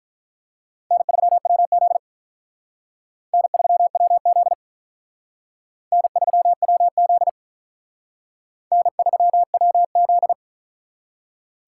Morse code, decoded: N3WZ N3WZ N3WZ N3WZ

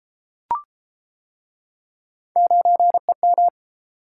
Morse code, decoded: 9EM

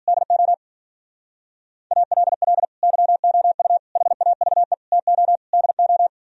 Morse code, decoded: DO ALL YOU HAVE TO DO

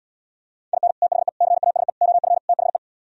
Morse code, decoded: IN THE CASE OF THE

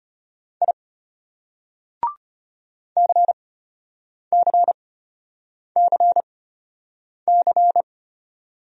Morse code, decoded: I CCCC

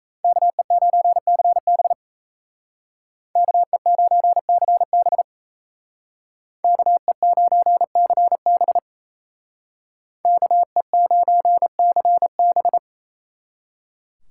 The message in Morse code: KE9CB KE9CB KE9CB KE9CB

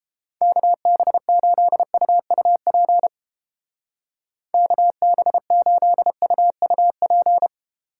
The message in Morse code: KB8UUP KB8UUP

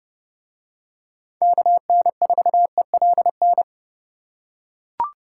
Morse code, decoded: KN4ELN

